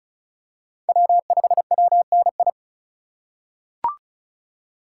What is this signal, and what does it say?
Morse code, heard as a keyed tone, about 36 wpm, 700 Hz. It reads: W5WNI
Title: USA Call Signs - 5 Characters - 4x Character Spacing - 35wpm